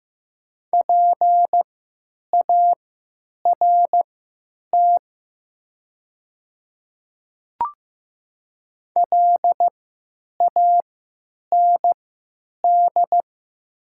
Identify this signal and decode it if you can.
Morse code, heard as a keyed tone, about 15 wpm, spaced as for 9 wpm, 700 Hz. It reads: PART LAND
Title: 3x Character Spacing - Top 200 Words - 15wpm